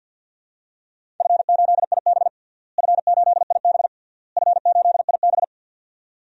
Morse code, heard as a keyed tone, about 51 wpm, 700 Hz. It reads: F8IB F8IB F8IB
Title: DXCC-100 Call Signs - 4 Characters - Repeated 3x - 50wpm